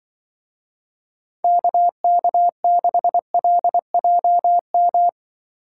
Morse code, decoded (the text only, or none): KK6LJM